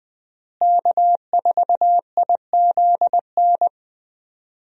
Morse code, heard as a keyed tone, about 20 wpm, 700 Hz. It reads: K4IZN
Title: USA Call Signs - 5 Characters - Repeated 2x - 20wpm